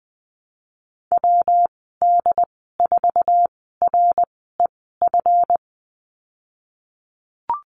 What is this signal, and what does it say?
Morse code, heard as a keyed tone, about 19 wpm, spaced as for 14 wpm, 700 Hz. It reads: WD4REF
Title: 167 Lesson 27g - Number 7 - Call Signs 20wpm